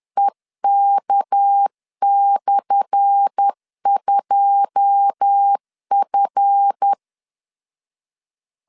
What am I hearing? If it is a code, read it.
Morse code, heard as a keyed tone, about 10 wpm, 800 Hz. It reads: EK/2F
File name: LEZIONI / 10 SIGNS / 10A SLASH 10wpm 2808